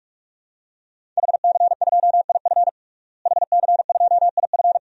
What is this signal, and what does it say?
Morse code, heard as a keyed tone, about 41 wpm, 700 Hz. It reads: HC2IF HC2IF
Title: DXCC-100 Call Signs - 5 Characters - Repeated 2x - 45wpm